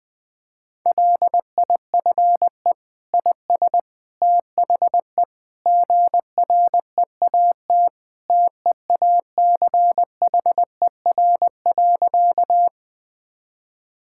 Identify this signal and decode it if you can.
Morse code, heard as a keyed tone, about 20 wpm, 700 Hz. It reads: LIFE IS THE GREAT TEACHER.
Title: Sentences from Top 1000 Words - Review Part 6 - 20wpm